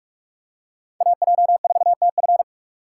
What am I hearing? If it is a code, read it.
Morse code, heard as a keyed tone, about 46 wpm, 700 Hz. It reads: AJ4TF
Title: K1USN SST QSO Practice - 45wpm